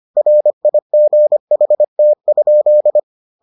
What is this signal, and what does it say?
Morse code, heard as a keyed tone, about 25 wpm, 600 Hz. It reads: RIGHT?